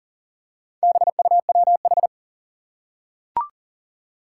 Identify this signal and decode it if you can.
Morse code, heard as a keyed tone, about 40 wpm, 700 Hz. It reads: BUWH
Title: Four Challenging Letter-Number Combinations 40wpm